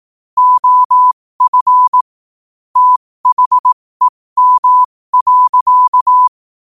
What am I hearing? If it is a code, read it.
Morse code, heard as a keyed tone, about 18 wpm, spaced as for 15 wpm, 1000 Hz. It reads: OF THEM.
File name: qotdfiles / 15wpm